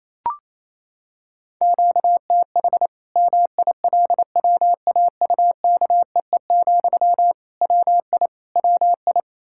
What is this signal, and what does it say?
Morse code, heard as a keyed tone, about 28 wpm, 700 Hz. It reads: QTH MILWAUKEE, WI WI
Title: K1USN SST QSO Practice 28wpm